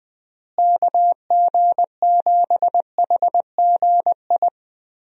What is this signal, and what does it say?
Morse code, heard as a keyed tone, about 20 wpm, 700 Hz. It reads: KG7HGI
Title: USA Call Signs - 6 Characters - Repeated 2x - 20wpm